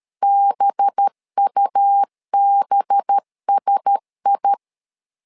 Morse code, heard as a keyed tone, about 13 wpm, 800 Hz. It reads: BUBSI
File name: LEZIONI / 5 LBVC / 5 LB 12wpm 1906